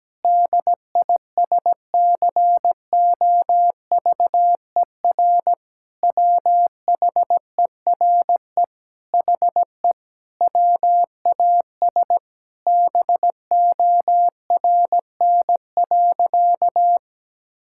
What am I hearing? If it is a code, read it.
Morse code, heard as a keyed tone, about 17 wpm, 700 Hz. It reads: DISCOVER WHERE HE WAS BORN.
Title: Sentences from Top 1600 Words 17wpm